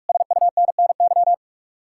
Morse code, heard as a keyed tone, about 58 wpm, 700 Hz. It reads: SUNNY